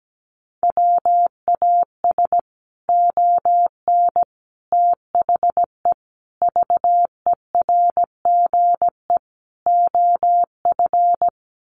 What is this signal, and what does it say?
Morse code, heard as a keyed tone, about 17 wpm, 700 Hz. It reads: WAS ON THE VERGE OF
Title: Sets of 5 Words Encore 17wpm